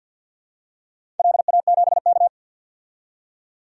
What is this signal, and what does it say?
Morse code, heard as a keyed tone, about 50 wpm, 700 Hz. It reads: LA7K
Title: DXCC-100 Call Signs - 4 Characters - 50wpm